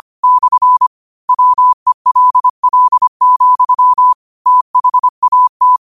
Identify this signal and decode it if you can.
Morse code, heard as a keyed tone, about 25 wpm, 1000 Hz. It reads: / WELL, THAT
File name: qotdfiles / 25wpm